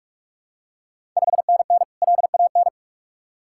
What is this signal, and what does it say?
Morse code, heard as a keyed tone, about 46 wpm, 700 Hz. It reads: 5NN LAN
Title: Ontario QSO Party Exchange 45wpm